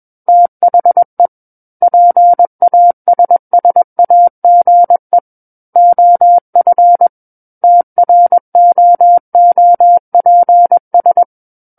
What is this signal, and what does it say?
Morse code, heard as a keyed tone, about 21 wpm, 700 Hz. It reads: THE PASSAGE OF TROOPS
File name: Book One / 12-Chapter 12 - 21 WPM